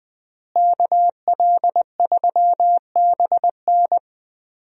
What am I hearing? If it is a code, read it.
Morse code, heard as a keyed tone, about 20 wpm, 700 Hz. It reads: KL3BN
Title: USA Call Signs - 5 Characters - Repeated 3x - 20wpm